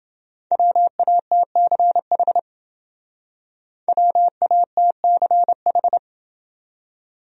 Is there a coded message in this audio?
Yes: Morse code from 0.5 to 6.0 s, heard as a keyed tone, about 29 wpm, 700 Hz. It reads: WATCH WATCH